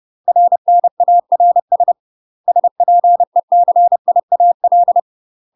Morse code, heard as a keyed tone, about 30 wpm, 700 Hz. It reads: RNARS SPECIAL
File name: UK MASTER QSOs TRAINING / 30WPM / UK MASTER QSOs 0018